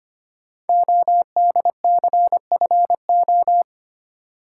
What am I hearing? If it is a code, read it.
Morse code, heard as a keyed tone, about 25 wpm, 700 Hz. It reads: ODCFO
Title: Code Groups - 5 Letters - 25wpm